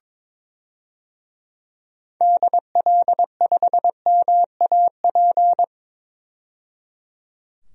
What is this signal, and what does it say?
Morse code, heard as a keyed tone, about 22 wpm, 700 Hz. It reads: DL5MAP
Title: DXCC-100 Call Signs - 6 Characters - 22wpm